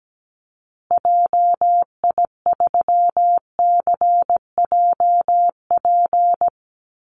Morse code, heard as a keyed tone, about 17 wpm, 700 Hz. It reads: JI3CJP
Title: Call Signs Repeated 2x - 17wpm